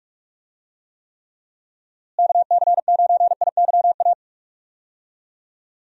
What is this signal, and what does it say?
Morse code, heard as a keyed tone, about 41 wpm, 700 Hz. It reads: KC9IYA
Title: USA Call Signs - 6 Characters - Repeated 3x - 45wpm